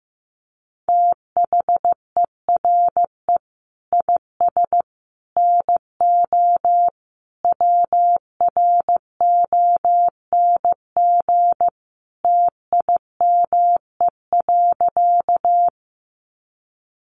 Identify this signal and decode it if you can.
Morse code, heard as a keyed tone, about 15 wpm, 700 Hz. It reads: THERE IS NO WRONG TIME.
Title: Sentences from Top 900 Words 15wpm